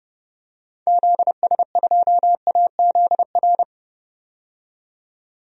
Morse code, heard as a keyed tone, about 32 wpm, 700 Hz. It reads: ZS2AZR
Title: DXCC-100 Call Signs - 6 Characters - 30wpm